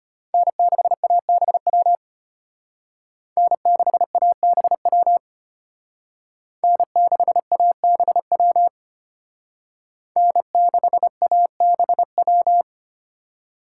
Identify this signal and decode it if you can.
Morse code, heard as a keyed tone, about 33 wpm, 700 Hz. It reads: N6ABW N6ABW N6ABW N6ABW